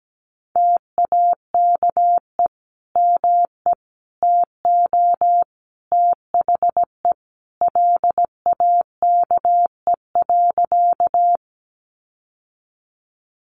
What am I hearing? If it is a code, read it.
Morse code, heard as a keyed tone, about 17 wpm, 700 Hz. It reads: TAKE ME TO THE LAKE.
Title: Sentences from Top 1100 Words 17wpm